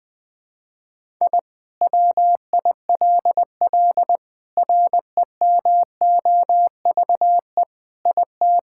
Morse code, heard as a keyed tone, about 19 wpm, 700 Hz. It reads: I WILL REMOVE IT
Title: Sentences from Top 1000 Words - Review Part 5 - 20wpm